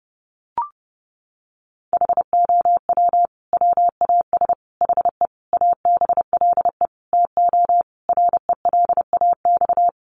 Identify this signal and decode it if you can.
Morse code, heard as a keyed tone, about 30 wpm, 700 Hz. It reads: HOW WAS HE ABLE TO RELAX
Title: Sentences from Top 1400 Words 30wpm